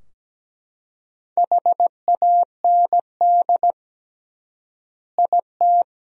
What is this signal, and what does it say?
Morse code, heard as a keyed tone, about 17 wpm, 700 Hz. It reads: HAND IT
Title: 3x Word-Spacing - Sentences from Top 200 Words - 17wpm